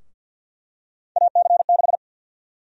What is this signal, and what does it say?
Morse code, heard as a keyed tone, about 52 wpm, 700 Hz. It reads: AC6
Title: USA Call Sign Prefixes - 50wpm